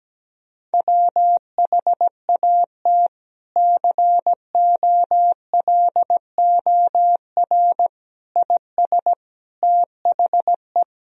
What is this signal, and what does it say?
Morse code, heard as a keyed tone, about 17 wpm, 700 Hz. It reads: WHAT COLOR IS THE
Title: Sentences from Top 1000 Words - Review Part 3 - 17wpm